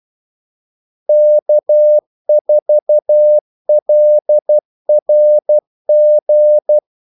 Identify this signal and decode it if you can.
Morse code, heard as a keyed tone, about 12 wpm, 600 Hz. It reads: K4LRG
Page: 2 ekstra ordmellemrum mellem kaldesignalerne